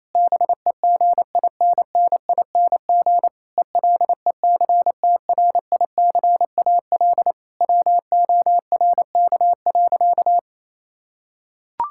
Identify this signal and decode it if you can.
Morse code, heard as a keyed tone, about 27 wpm, 700 Hz. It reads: BEGINNING ELECTRICAL WORK.
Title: Sentences from Top 1000 Words - Review Part 6 - 28wpm